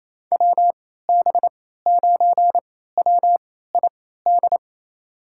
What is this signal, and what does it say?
Morse code, heard as a keyed tone, about 28 wpm, spaced as for 16 wpm, 700 Hz. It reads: WB9WID
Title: USA Call Signs - 6 Characters - 3x Character Spacing - 28wpm